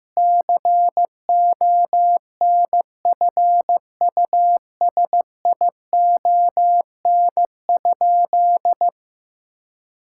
Morse code, heard as a keyed tone, about 15 wpm, 700 Hz. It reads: CONFUSION?
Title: Sentences from Top 1600 Words 15wpm